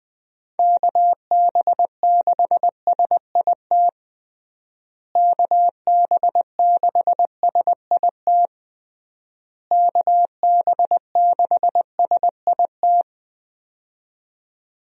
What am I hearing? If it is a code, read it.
Morse code, heard as a keyed tone, about 21 wpm, 700 Hz. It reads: KB6SIT KB6SIT KB6SIT
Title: USA Call Signs - 6 Characters - Repeated 3x - 20wpm